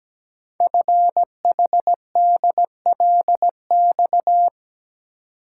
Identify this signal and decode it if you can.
Morse code, heard as a keyed tone, about 17 wpm, 700 Hz. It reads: FHDLX
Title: Code Groups - 5 Letter:Numbers - 17wpm